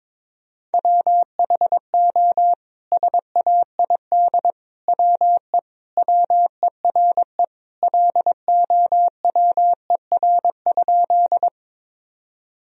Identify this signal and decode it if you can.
Morse code, heard as a keyed tone, about 22 wpm, 700 Hz. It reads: WHO SAID WE WERE LOWER?
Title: Sentences from Top 1000 Words - Review Part 4 - 22wpm